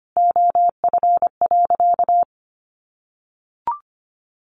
Morse code, heard as a keyed tone, about 25 wpm, 700 Hz. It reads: OF.
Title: Sentences from Top 1500 Words 25wpm